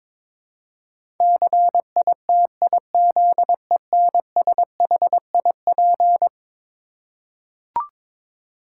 Morse code, heard as a keyed tone, about 22 wpm, 700 Hz. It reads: CITIZENSHIP
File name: practice / 22wpm / Eleven Letter Words - 22wpm